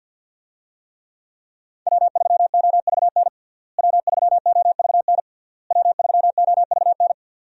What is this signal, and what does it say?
Morse code, heard as a keyed tone, about 52 wpm, 700 Hz. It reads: W3OVN W3OVN W3OVN
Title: USA Call Signs - 5 Characters - Repeated 3x - 50wpm